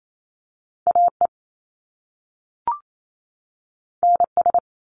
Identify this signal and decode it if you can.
Morse code, heard as a keyed tone, about 28 wpm, 700 Hz. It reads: AE NS
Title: Top Two Letter Combinations in Words 28wpm